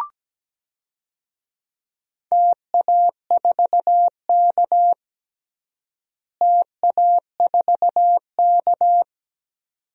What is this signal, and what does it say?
Morse code, heard as a keyed tone, about 17 wpm, 700 Hz. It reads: TA4K TA4K
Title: DXCC-100 Call Signs - 4 Characters - Repeated 2x - 17wpm